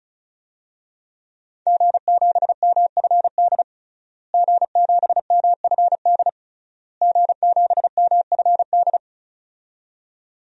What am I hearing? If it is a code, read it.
Morse code, heard as a keyed tone, about 36 wpm, 700 Hz. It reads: G7MFD G7MFD G7MFD
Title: DXCC-100 Call Signs - 5 Characters - Repeated 3x - 35wpm